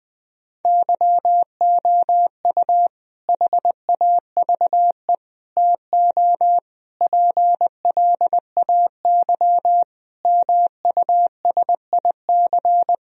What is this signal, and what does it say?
Morse code, heard as a keyed tone, about 20 wpm, 700 Hz. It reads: YOU HAVE TO PLAY MUSIC
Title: Sentences from Top 1000 Words - Review Part 1 - 20wpm